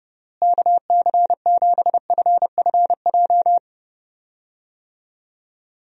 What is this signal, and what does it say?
Morse code, heard as a keyed tone, about 32 wpm, 700 Hz. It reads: KC7FFJ
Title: USA Call Signs - 6 Characters - Repeated 2x - 30wpm